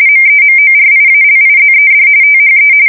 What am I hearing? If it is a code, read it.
RTTY, about 45.45 baud. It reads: MEMORIALIZED THE
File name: fsk170 / USS-Missouri